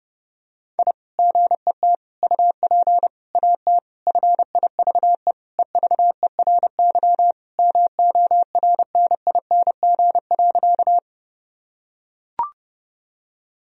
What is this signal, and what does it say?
Morse code, heard as a keyed tone, about 30 wpm, 700 Hz. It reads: I GET UP AT FIVE EVERY MORNING.
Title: Sentences from Top 1000 Words - Review Part 2 - 30wpm